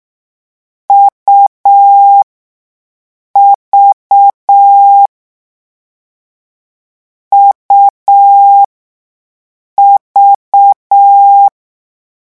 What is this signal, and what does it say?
Morse code, heard as a keyed tone, about 6 wpm, spaced as for 5 wpm, 800 Hz. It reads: UV UV
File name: morse mp3 / lecon3-vitesse 6